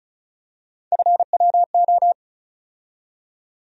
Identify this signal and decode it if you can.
Morse code, heard as a keyed tone, about 36 wpm, 700 Hz. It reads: FWO
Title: USA Call Sign Postfixes - 3 Letters 35wpm